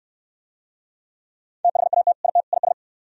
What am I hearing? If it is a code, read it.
Morse code, heard as a keyed tone, about 62 wpm, 700 Hz. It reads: THAT IT IS